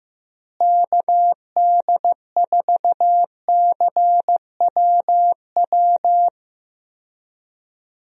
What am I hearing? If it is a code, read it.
Morse code, heard as a keyed tone, about 15 wpm, 700 Hz. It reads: KD4CWW
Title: USA Call Signs - 6 Characters - Repeated 3x - 15wpm